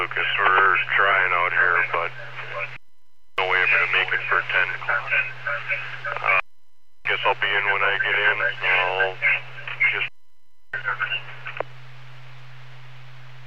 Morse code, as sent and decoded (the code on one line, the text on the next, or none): none